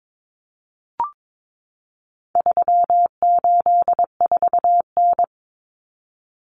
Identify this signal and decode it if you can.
Morse code, heard as a keyed tone, about 22 wpm, 700 Hz. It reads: 384N
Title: Common Cut Numbers - 4-Digit 22wpm